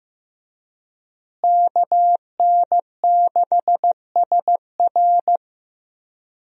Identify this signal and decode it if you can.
Morse code, heard as a keyed tone, about 15 wpm, 700 Hz. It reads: KN6SR
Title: USA Call Signs - 5 Characters - Repeated 2x - 15wpm